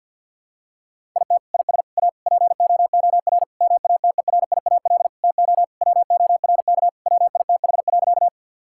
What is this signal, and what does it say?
Morse code, heard as a keyed tone, about 50 wpm, 700 Hz. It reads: IT IS A POOR MATERIAL TO WORK WITH.
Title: Sentences from Top 1000 Words - Review Part 5 - 50wpm